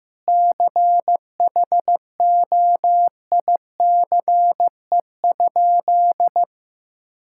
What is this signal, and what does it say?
Morse code, heard as a keyed tone, about 15 wpm, 700 Hz. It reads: CHOICE?
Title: Sentences from Top 1000 Words - Review Part 6 - 15wpm